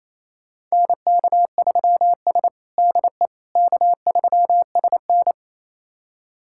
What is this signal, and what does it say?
Morse code, heard as a keyed tone, about 28 wpm, 700 Hz. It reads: NK3S DE K3SN
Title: USA Callsign DE Callsign - 4 Characters - 28wpm